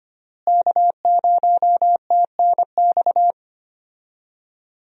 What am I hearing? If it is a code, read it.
Morse code, heard as a keyed tone, about 25 wpm, 700 Hz. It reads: K0TNX
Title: USA Call Signs - 5 Characters - Repeated 3x - 25wpm